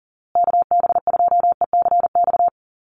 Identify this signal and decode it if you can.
Morse code, heard as a keyed tone, about 40 wpm, 700 Hz. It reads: KB2ECX
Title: USA Call Signs Repeated 3x 40wpm